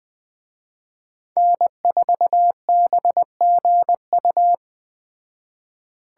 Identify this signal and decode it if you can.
Morse code, heard as a keyed tone, about 20 wpm, 700 Hz. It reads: N4BGU